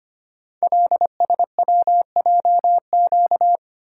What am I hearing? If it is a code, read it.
Morse code, heard as a keyed tone, about 25 wpm, 700 Hz. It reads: LSWJQ
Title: Code Groups - 5 Letter:Numbers - 25wpm